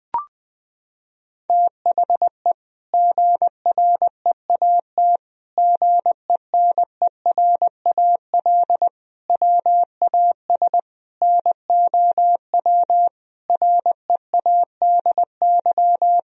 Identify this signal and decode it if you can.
Morse code, heard as a keyed tone, about 21 wpm, 700 Hz. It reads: THE GREAT GENERAL WAS NOW READY
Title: Sentences from Top 1000 Words - Review Part 4 - 20wpm